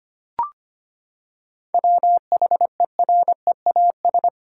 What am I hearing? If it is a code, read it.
Morse code, heard as a keyed tone, about 25 wpm, 700 Hz. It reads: WHEREAS
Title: Seven Letter Words - 25wpm